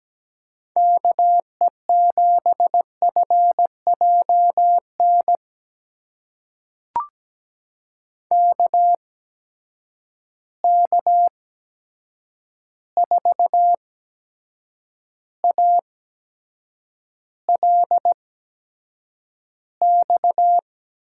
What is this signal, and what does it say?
Morse code, heard as a keyed tone, about 17 wpm, spaced as for 7 wpm, 700 Hz. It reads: KE7FJN KK4ALX
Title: USA Call Signs - 6 Characters - 8x Character Spacing - 17wpm